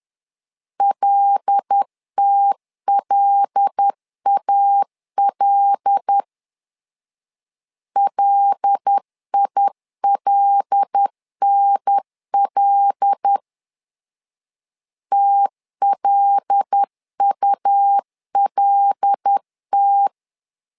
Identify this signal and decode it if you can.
Morse code, heard as a keyed tone, about 10 wpm, 800 Hz. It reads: LTLAL LILNL TLULT
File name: LEZIONI / 5 LBVC / 5 L 10wpm 2259